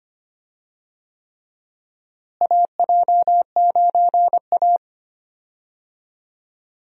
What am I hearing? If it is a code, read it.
Morse code, heard as a keyed tone, about 25 wpm, 700 Hz. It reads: AJ9A
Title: USA Call Signs - 4 Characters - Repeated 3x - 25wpm